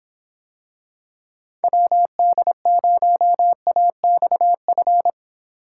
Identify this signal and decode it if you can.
Morse code, heard as a keyed tone, about 26 wpm, 700 Hz. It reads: WD0AXF